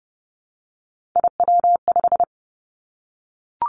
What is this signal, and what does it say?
Morse code, heard as a keyed tone, about 30 wpm, 700 Hz. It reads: IW5
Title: Three Challenging Letter-Number Combinations 30wpm